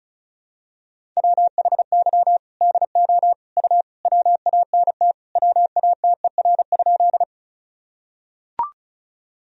Morse code, heard as a keyed tone, about 34 wpm, 700 Hz. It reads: WHY DO U WANT WATER?
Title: Sentences from Top 200 Words and Abbreviations 35wpm